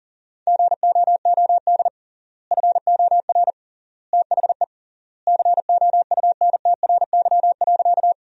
Morse code, heard as a keyed tone, about 37 wpm, 700 Hz. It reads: GOOD FOR THE COUNTRY.
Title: 3x Word-Spacing - Sentences from Top 300 Words - 40wpm